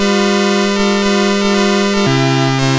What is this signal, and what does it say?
There is no code.